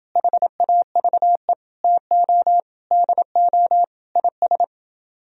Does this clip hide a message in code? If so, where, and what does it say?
Yes: Morse code from 0.2 to 4.6 s, heard as a keyed tone, about 30 wpm, 700 Hz. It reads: HAVE TO DO IS